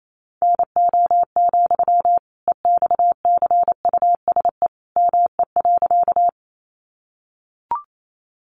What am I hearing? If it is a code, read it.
Morse code, heard as a keyed tone, about 28 wpm, 700 Hz. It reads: NO, EXCUSE ME.
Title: Sentences from Top 700 Words 28wpm